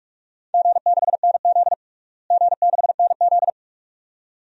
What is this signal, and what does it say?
Morse code, heard as a keyed tone, about 46 wpm, 700 Hz. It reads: G6NZ G6NZ
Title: DXCC-100 Call Signs - 4 Characters - Repeated 2x - 45wpm